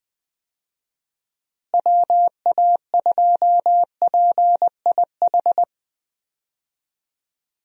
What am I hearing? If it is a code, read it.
Morse code, heard as a keyed tone, about 20 wpm, 700 Hz. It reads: WA2PIH
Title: USA Call Signs - 6 Characters - Repeated 3x - 20wpm